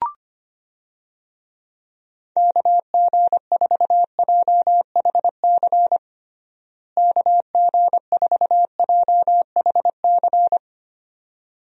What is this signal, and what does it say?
Morse code, heard as a keyed tone, about 25 wpm, 700 Hz. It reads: KG4JHC KG4JHC
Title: USA Call Signs - 6 Characters - Repeated 2x - 25wpm